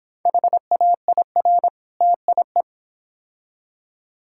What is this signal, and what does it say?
Morse code, heard as a keyed tone, about 26 wpm, 700 Hz. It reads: HAIR TIE